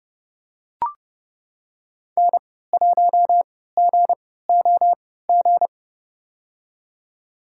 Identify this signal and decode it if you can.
Morse code, heard as a keyed tone, about 30 wpm, spaced as for 17 wpm, 700 Hz. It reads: N1GOG
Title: USA Call Signs - 5 Characters - 3x Character Spacing - 30wpm